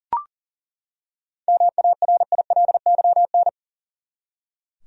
Morse code, heard as a keyed tone, about 40 wpm, 700 Hz. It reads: MARILYN
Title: USA Common Names - 40wpm